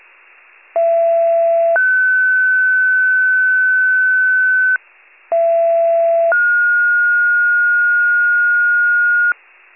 Two-tone page, heard as a one-second tone, then a three-second tone, about 670 then 1500 Hz, about 670 then 1430 Hz. It is heard from 0.8 to 9.3 s.